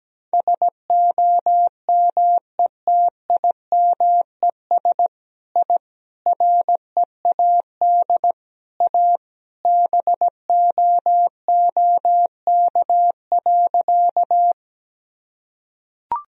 Morse code, heard as a keyed tone, about 17 wpm, 700 Hz. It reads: SOMETIMES I READ A BOOK.
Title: Sentences from Top 1000 Words - Review Part 3 - 17wpm